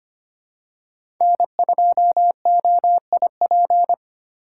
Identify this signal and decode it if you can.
Morse code, heard as a keyed tone, about 25 wpm, 700 Hz. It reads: N2OIP
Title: USA Call Signs - 5 Characters - 4x Character Spacing - 25wpm